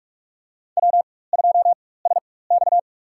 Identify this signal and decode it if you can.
Morse code, heard as a keyed tone, about 43 wpm, spaced as for 21 wpm, 700 Hz. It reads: W2SX
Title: USA Call Signs - 4x Character Spacing - 45wpm